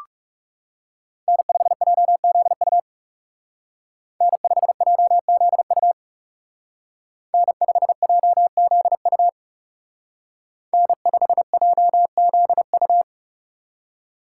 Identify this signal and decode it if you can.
Morse code, heard as a keyed tone, about 36 wpm, 700 Hz. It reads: N5JZU N5JZU N5JZU N5JZU
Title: Speed Racing - USA 1x3 Call Signs 30wpm